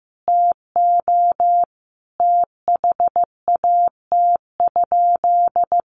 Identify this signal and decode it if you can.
Morse code, heard as a keyed tone, about 14 wpm, 700 Hz. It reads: TO THAT?
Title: Sentences from Top 1100 Words 15wpm